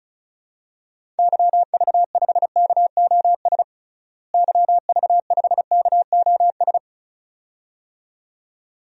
Morse code, heard as a keyed tone, about 36 wpm, 700 Hz. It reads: YV5KOS YV5KOS
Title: DXCC-100 Call Signs - 6 Characters - Repeated 2x - 35wpm